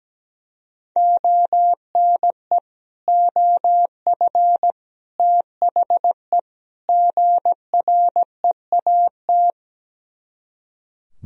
Morse code, heard as a keyed tone, about 17 wpm, 700 Hz. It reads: ONE OF THE GREAT
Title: Speed-Racing - Sets of 4 Words 17wpm